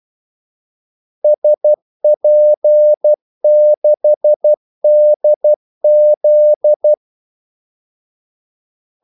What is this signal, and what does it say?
Morse code, heard as a keyed tone, about 12 wpm, 600 Hz. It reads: SP6DZ
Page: 2 ekstra ordmellemrum mellem kaldesignalerne